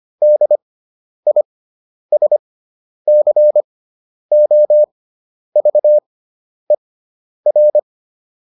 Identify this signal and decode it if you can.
Morse code, heard as a keyed tone, about 25 wpm, spaced as for 10 wpm, 600 Hz. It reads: DISCOVER